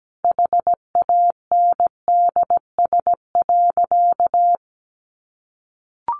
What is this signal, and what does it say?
Morse code, heard as a keyed tone, about 17 wpm, 700 Hz. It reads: HANDS.
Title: Easy Phrases 17wpm